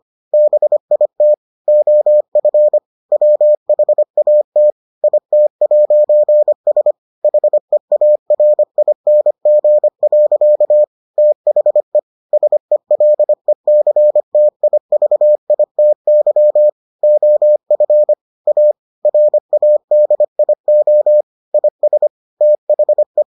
Morse code, heard as a keyed tone, about 25 wpm, 600 Hz. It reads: BIT OF WHAT IT'S HEARING. THE SELECTIVITY OF A RADIO IS THE